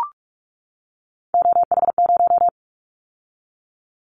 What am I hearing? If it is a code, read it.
Morse code, heard as a keyed tone, about 44 wpm, 700 Hz. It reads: OH0
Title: Callsign Prefixes for Top 100 DXCC - 45wpm